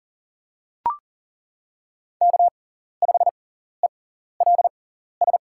Morse code, heard as a keyed tone, about 38 wpm, spaced as for 14 wpm, 700 Hz. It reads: K5ELS
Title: USA Call Signs - 5 Characters - 6x Character Spacing - 40wpm